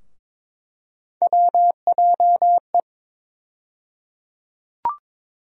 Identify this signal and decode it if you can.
Morse code, heard as a keyed tone, about 22 wpm, 700 Hz. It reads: WJE
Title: USA Call Sign Postfixes - 3 Letters 22wpm